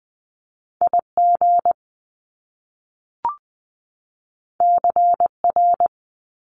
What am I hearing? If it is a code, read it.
Morse code, heard as a keyed tone, about 20 wpm, 700 Hz. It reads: IG CR
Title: Top Two Letter Combinations in Words 20wpm